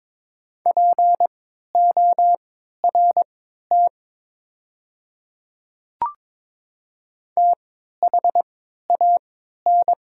Morse code, heard as a keyed tone, about 23 wpm, spaced as for 13 wpm, 700 Hz. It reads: PORT THAN
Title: 3x Character Spacing - Top 300 Words - 22wpm